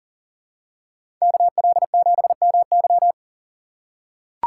Morse code, heard as a keyed tone, about 40 wpm, 700 Hz. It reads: KL7MY
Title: USA Call Signs - 5 Characters - 2x Character Spacing - 40wpm